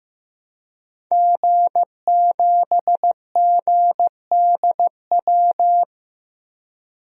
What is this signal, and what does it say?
Morse code, heard as a keyed tone, about 15 wpm, 700 Hz. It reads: G7GDW
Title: DXCC-100 Call Signs - 5 Characters - Repeated 2x - 15wpm